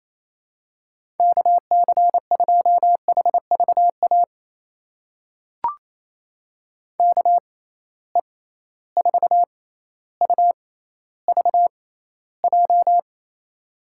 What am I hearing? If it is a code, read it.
Morse code, heard as a keyed tone, about 28 wpm, spaced as for 14 wpm, 700 Hz. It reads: KC2HVA KE4UVJ
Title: USA Call Signs - 6 Characters - 6x Character Spacing - 28wpm